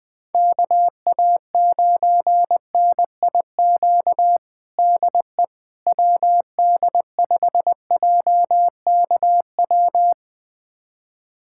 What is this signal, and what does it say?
Morse code, heard as a keyed tone, about 20 wpm, 700 Hz. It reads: KA9NIQ DE WD5JKW
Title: USA Callsign DE Callsign - 6 Characters - 20wpm